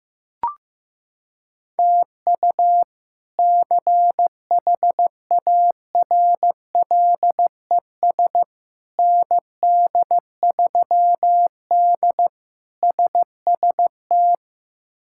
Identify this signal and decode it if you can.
Morse code, heard as a keyed tone, about 16 wpm, 700 Hz. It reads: TU CHARLES ND3D SST
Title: K1USN SST QSO Practice - 15wpm